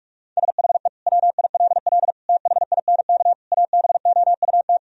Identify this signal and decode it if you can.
Morse code, heard as a keyed tone, about 44 wpm, 700 Hz. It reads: SHE WILL THINK ABOUT
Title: Sentences from Top 1000 Words - Review Part 1 - 45wpm